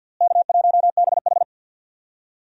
Morse code, heard as a keyed tone, about 49 wpm, 700 Hz. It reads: K1BH
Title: USA Call Signs - 4 Characters - Repeated 2x - 50wpm